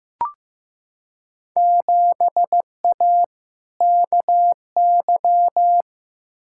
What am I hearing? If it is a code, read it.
Morse code, heard as a keyed tone, about 15 wpm, 700 Hz. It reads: 7A KY
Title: Field Day Exchanges Encore - 15wpm